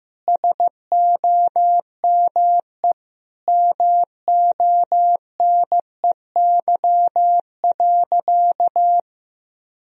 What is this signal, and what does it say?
Morse code, heard as a keyed tone, about 15 wpm, 700 Hz. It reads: SOME MONEY.
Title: Sentences from Top 1000 Words - Review Part 3 - 15wpm